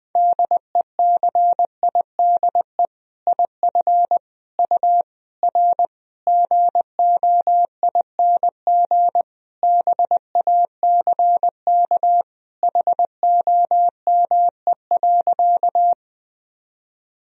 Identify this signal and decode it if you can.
Morse code, heard as a keyed tone, about 20 wpm, 700 Hz. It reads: DECIDE IF U R GOING BACK HOME.
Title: Sentences from Top 500 Words and Abbreviations 20wpm